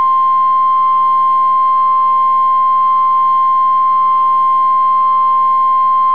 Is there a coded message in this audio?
Yes: Emergency Alert System (SAME) tones (the 1050 Hz attention tone) from 0.5 to 5.7 s.